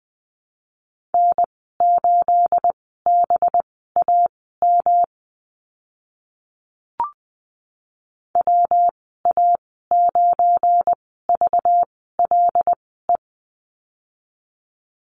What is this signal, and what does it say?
Morse code, heard as a keyed tone, about 19 wpm, spaced as for 14 wpm, 700 Hz. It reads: N8BAM WA9VLE
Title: 219 Lesson 35g - Number 0 - Call Signs 20wpm